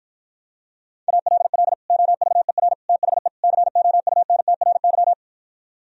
Morse code, heard as a keyed tone, about 58 wpm, 700 Hz. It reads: ALL OVER THE COUNTRY